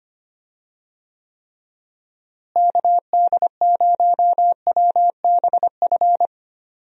Morse code, heard as a keyed tone, about 25 wpm, 700 Hz. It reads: KD0WBF
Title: USA Call Signs - 6 Characters - Repeated 3x - 25wpm